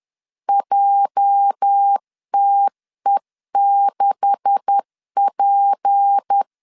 Morse code, heard as a keyed tone, about 10 wpm, 800 Hz. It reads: JTE6P
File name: LEZIONI / 9 286 10wpm 2735